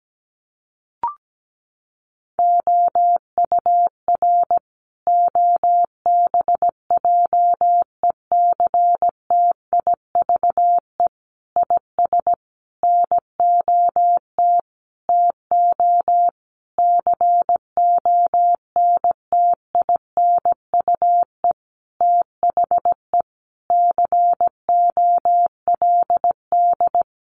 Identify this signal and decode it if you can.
Morse code, heard as a keyed tone, about 17 wpm, 700 Hz. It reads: OUR OBJECTIVE IS NOT TO CONTINUE THE COLD
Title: Sentences from Top 1200 Words 17wpm